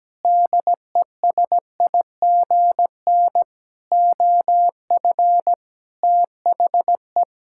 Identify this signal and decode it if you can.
Morse code, heard as a keyed tone, about 17 wpm, 700 Hz. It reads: DESIGN OF THE
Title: Sentences from Top 1000 Words - Review Part 5 - 17wpm